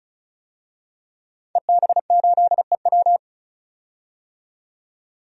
Morse code, heard as a keyed tone, about 36 wpm, 700 Hz. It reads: EB8EW
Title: DXCC-100 Call Signs - 5 Characters - 35wpm